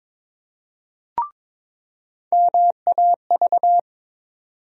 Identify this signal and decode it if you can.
Morse code, heard as a keyed tone, about 22 wpm, 700 Hz. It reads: MAV
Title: USA Call Sign Postfixes - 3 Letters 22wpm